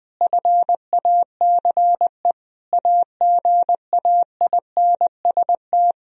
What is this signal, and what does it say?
Morse code, heard as a keyed tone, about 20 wpm, 700 Hz. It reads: FACE AGAINST
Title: Sentences from Top 1000 Words - Review Part 2 - 20wpm